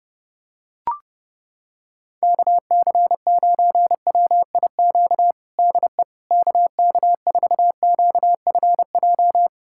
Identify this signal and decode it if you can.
Morse code, heard as a keyed tone, about 32 wpm, 700 Hz. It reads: KC9WIQ DE KK4QFJ
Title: USA Callsign DE Callsign - 6 Characters - 30wpm